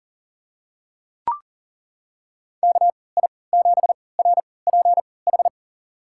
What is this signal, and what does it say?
Morse code, heard as a keyed tone, about 40 wpm, spaced as for 23 wpm, 700 Hz. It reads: KI7RPH
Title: USA Call Signs - 6 Characters - 3x Character Spacing - 40wpm